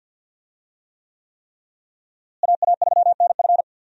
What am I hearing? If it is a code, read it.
Morse code, heard as a keyed tone, about 50 wpm, 700 Hz. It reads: AA3NF
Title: USA Call Signs - 5 Characters - Repeated 3x - 50wpm